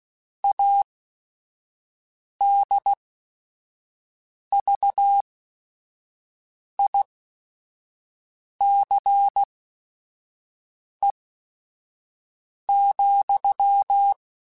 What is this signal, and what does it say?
Morse code, heard as a keyed tone, about 16 wpm, spaced as for 5 wpm, 800 Hz. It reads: ADVICE,
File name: Quotes / Quotes-mp3s / Advice @5